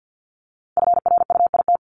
Morse code, heard as a keyed tone, about 50 wpm, 700 Hz. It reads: FRUIT
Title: Multi-letter Phonemes Repeated 3x 50wpm